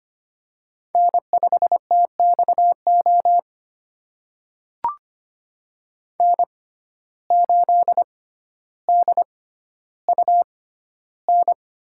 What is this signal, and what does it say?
Morse code, heard as a keyed tone, about 25 wpm, spaced as for 13 wpm, 700 Hz. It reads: N5TXO N8DUN
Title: USA Call Signs - 5 Characters - 6x Character Spacing - 25wpm